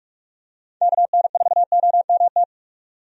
Morse code, heard as a keyed tone, about 46 wpm, 700 Hz. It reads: KN4OMT
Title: USA Call Signs - 6 Characters - 45wpm